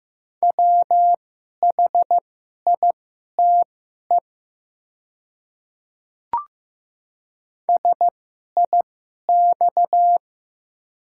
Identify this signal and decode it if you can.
Morse code, heard as a keyed tone, about 15 wpm, spaced as for 11 wpm, 700 Hz. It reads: WHITE SIX
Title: 2x Character Spacing - Top 500 Words - 15wpm